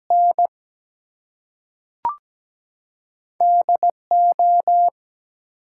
Morse code, heard as a keyed tone, about 17 wpm, 700 Hz. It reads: N DO
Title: Two Letter Words - 17wpm